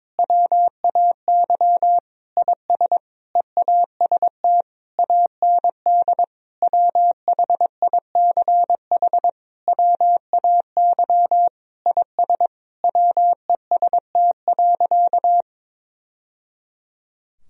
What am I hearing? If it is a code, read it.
Morse code, heard as a keyed tone, about 22 wpm, 700 Hz. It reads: WAY IS EAST AND WHICH WAY IS WEST.
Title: Sentences from Top 1000 Words - Review Part 3 - 22wpm